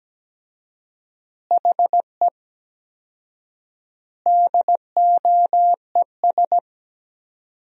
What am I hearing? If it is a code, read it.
Morse code, heard as a keyed tone, about 17 wpm, 700 Hz. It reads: HE DOES
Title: 4x Word-Spacing - Sentences from Top 200 Words - 17wpm